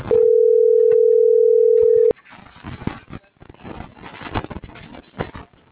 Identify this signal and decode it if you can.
Ringback tone (North America), 440 + 480 Hz.